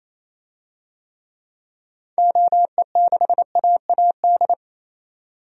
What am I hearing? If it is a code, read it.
Morse code, heard as a keyed tone, about 28 wpm, 700 Hz. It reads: OE6AAD
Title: DXCC-100 Call Signs - 6 Characters - 28wpm